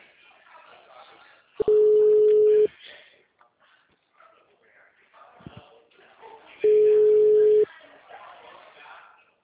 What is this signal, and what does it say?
Ringback tone (Europe, 425 Hz), with speech.